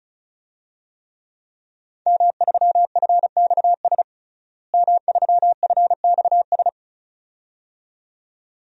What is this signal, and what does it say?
Morse code, heard as a keyed tone, about 34 wpm, 700 Hz. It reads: M3FXS M3FXS
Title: DXCC-100 Call Signs - 5 Characters - Repeated 2x - 35wpm